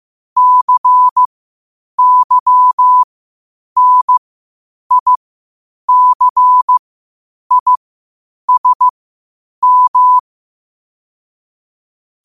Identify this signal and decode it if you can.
Morse code, heard as a keyed tone, about 15 wpm, spaced as for 8 wpm, 1000 Hz. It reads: CYNICISM
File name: qotdfiles / 8wpm